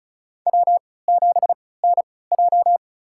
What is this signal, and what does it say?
Morse code, heard as a keyed tone, about 36 wpm, spaced as for 20 wpm, 700 Hz. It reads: W7NJ
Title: USA Call Signs - 3x Character Spacing - 35wpm